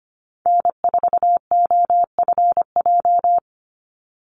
Morse code, heard as a keyed tone, about 25 wpm, 700 Hz. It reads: N4OFJ
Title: USA Call Signs 25wpm